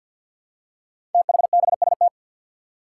Morse code, heard as a keyed tone, about 49 wpm, 700 Hz. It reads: THBST
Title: Code Groups - 5 Letters - 50wpm